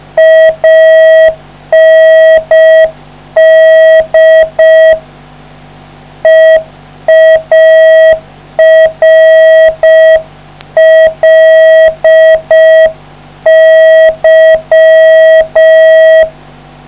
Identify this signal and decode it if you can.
Morse code, heard as a keyed tone, about 5 wpm, 640 Hz. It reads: AND EARLY